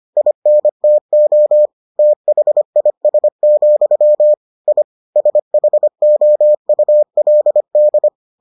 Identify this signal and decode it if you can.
Morse code, heard as a keyed tone, about 25 wpm, 600 Hz. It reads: INTO THIS, I SHOULD